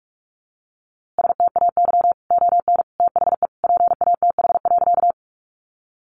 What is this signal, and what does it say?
Morse code, heard as a keyed tone, about 44 wpm, 700 Hz. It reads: STAY ON THE PATH.
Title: A Sound Of Thunder 45wpm